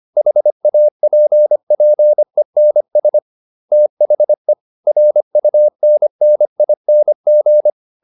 Morse code, heard as a keyed tone, about 25 wpm, 600 Hz. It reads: HAPPENS THE RUNNING